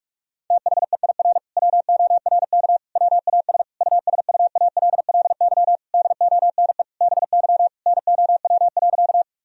Morse code, heard as a keyed tone, about 46 wpm, 700 Hz. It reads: THEIR WORK WAS USUALLY DONE BY NOW.
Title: Sentences from Top 1000 Words - Review Part 3 - 45wpm